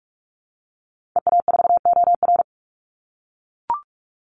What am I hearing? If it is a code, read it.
Morse code, heard as a keyed tone, about 44 wpm, 700 Hz. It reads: EA4OR
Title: Call Signs - 45wpm